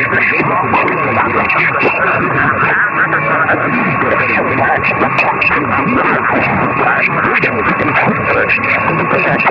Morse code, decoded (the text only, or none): none